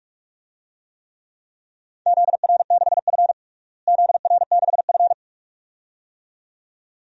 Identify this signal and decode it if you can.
Morse code, heard as a keyed tone, about 46 wpm, 700 Hz. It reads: ZR6F ZR6F
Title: DXCC-100 Call Signs - 4 Characters - Repeated 2x - 45wpm